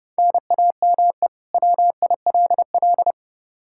Morse code, heard as a keyed tone, about 30 wpm, 700 Hz. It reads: NAME WILL
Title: K1USN SST QSO Practice 30wpm